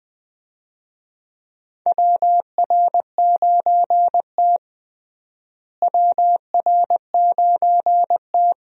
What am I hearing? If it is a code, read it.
Morse code, heard as a keyed tone, about 20 wpm, 700 Hz. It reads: WR9T WR9T
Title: USA Call Signs - 4 Characters - Repeated 2x - 20wpm